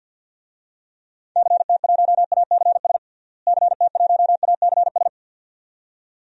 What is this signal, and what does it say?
Morse code, heard as a keyed tone, about 51 wpm, 700 Hz. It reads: CT1ACS CT1ACS
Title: DXCC-100 Call Signs - 6 Characters - Repeated 2x - 50wpm